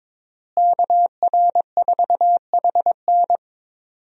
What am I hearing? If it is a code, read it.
Morse code, heard as a keyed tone, about 22 wpm, 700 Hz. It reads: KR4HN
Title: USA Call Signs - 5 Characters - Repeated 3x - 22wpm